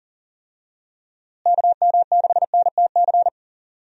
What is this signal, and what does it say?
Morse code, heard as a keyed tone, about 40 wpm, 700 Hz. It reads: KM6NTC
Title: USA Call Signs - 6 Characters - Repeated 3x - 40wpm